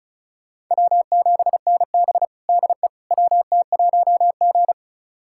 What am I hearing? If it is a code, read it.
Morse code, heard as a keyed tone, about 34 wpm, 700 Hz. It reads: W7NB DE WT1G
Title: USA Callsign DE Callsign - 4 Characters - 35wpm